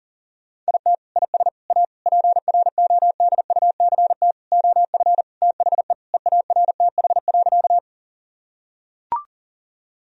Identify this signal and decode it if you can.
Morse code, heard as a keyed tone, about 40 wpm, 700 Hz. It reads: IT IS A PRODUCT OF THE EARTH.
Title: Sentences from Top 1000 Words - Review Part 2 - 40wpm